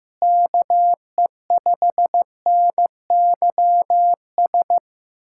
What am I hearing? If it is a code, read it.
Morse code, heard as a keyed tone, about 15 wpm, 700 Hz. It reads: KE5NYS
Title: USA Callsign DE Callsign 15wpm